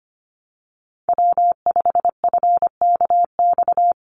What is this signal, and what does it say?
Morse code, heard as a keyed tone, about 25 wpm, 700 Hz. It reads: W5FKX
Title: Call Signs - 25wpm